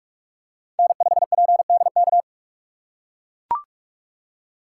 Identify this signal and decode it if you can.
Morse code, heard as a keyed tone, about 42 wpm, 700 Hz. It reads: N5PDK
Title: USA Call Signs - 5 Characters - 6x Character Spacing - 45wpm